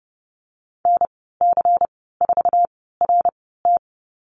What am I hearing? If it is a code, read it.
Morse code, heard as a keyed tone, about 30 wpm, spaced as for 17 wpm, 700 Hz. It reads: NC4RT
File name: Lesson 19g - Letter w - Call Signs 30wpm